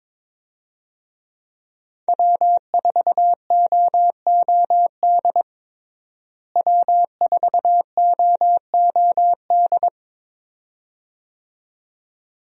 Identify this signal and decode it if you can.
Morse code, heard as a keyed tone, about 21 wpm, 700 Hz. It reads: W4OOD W4OOD
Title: USA Call Signs - 5 Characters - Repeated 2x - 22wpm